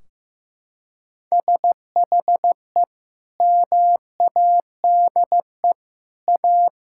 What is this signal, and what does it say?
Morse code, heard as a keyed tone, about 15 wpm, 700 Hz. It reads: SHE MADE A
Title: Sentences from Top 1600 Words 15wpm